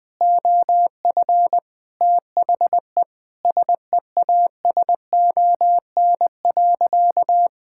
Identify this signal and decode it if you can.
Morse code, heard as a keyed tone, about 20 wpm, 700 Hz. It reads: OF THE SEASON.
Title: Sentences with 6-Word Phrases - Spoken Beginning - 20wpm